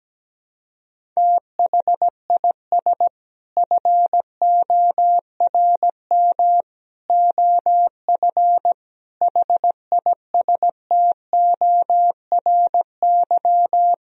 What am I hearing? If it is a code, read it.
Morse code, heard as a keyed tone, about 17 wpm, 700 Hz. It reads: THIS FORM OF HISTORY
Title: Sentences from Top 1000 Words - Review Part 4 - 17wpm